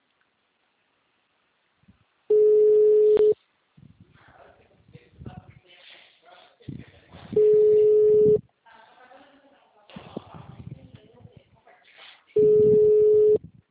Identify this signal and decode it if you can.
Ringback tone (Europe, 425 Hz), with speech.